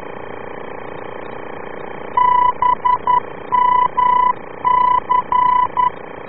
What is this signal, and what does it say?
Morse code, heard as a keyed tone, about 11 wpm, 990 Hz. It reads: BMC